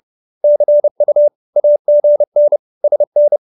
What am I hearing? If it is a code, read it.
Morse code, heard as a keyed tone, about 30 wpm, 590 Hz. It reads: CU AGN SN
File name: QSO317 30